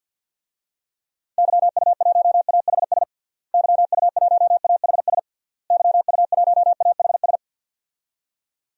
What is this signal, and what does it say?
Morse code, heard as a keyed tone, about 52 wpm, 700 Hz. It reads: YU1AHS YU1AHS YU1AHS
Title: DXCC-100 Call Signs - 6 Characters - Repeated 3x - 50wpm